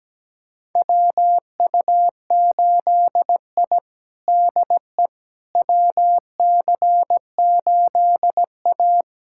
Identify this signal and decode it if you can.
Morse code, heard as a keyed tone, about 17 wpm, 700 Hz. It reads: WU8I DE WC8A
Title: USA Callsign DE Callsign - 4 Characters - 17wpm